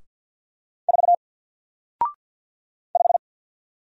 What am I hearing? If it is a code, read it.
Morse code, heard as a keyed tone, about 50 wpm, 700 Hz. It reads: 4 5
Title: Single Letter-Number Alt 50wpm